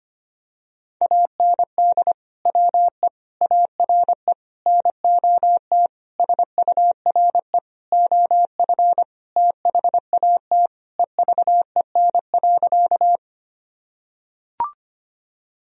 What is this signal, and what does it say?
Morse code, heard as a keyed tone, about 25 wpm, 700 Hz. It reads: AND WE ARE NOT SURE OF THAT EVEN.
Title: Sentences from Top 1000 Words - Review Part 1 - 25wpm